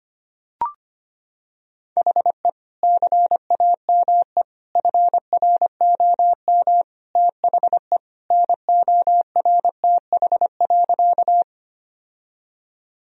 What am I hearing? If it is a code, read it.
Morse code, heard as a keyed tone, about 25 wpm, 700 Hz. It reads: HE CAME FROM THE NORTH.
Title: Sentences from Top 1000 Words - Review Part 1 - 25wpm